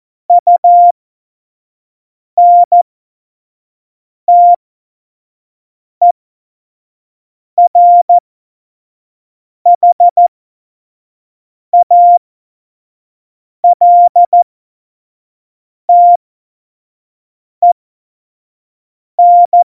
Morse code, heard as a keyed tone, about 14 wpm, spaced as for 5 wpm, 700 Hz. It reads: UNTERHALTEN
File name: Morsekurs / Witze / Witz 10.Juli 2018, 25 F70